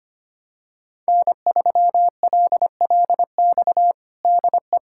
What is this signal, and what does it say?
Morse code, heard as a keyed tone, about 25 wpm, 700 Hz. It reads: N3LLX DE
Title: USA Callsign DE Callsign 25wpm